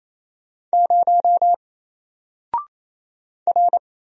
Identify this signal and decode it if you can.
Morse code, heard as a keyed tone, about 28 wpm, 700 Hz. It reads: 0 R
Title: Single Letter-Number Alt 28wpm